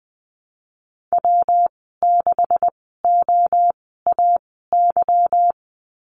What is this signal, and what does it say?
Morse code, heard as a keyed tone, about 20 wpm, spaced as for 15 wpm, 700 Hz. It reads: W6OAY
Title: 219 Lesson 35g - Number 0 - Call Signs 20wpm